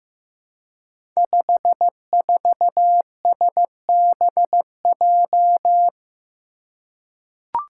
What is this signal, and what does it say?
Morse code, heard as a keyed tone, about 15 wpm, 700 Hz. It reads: 54SBJ
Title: Five Challenging Letter-Number Combinations 15wpm